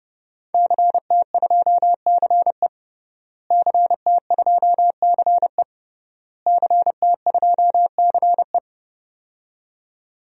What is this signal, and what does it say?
Morse code, heard as a keyed tone, about 30 wpm, 700 Hz. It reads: CT2CE CT2CE CT2CE
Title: DXCC-100 Call Signs - 5 Characters - Repeated 3x - 30wpm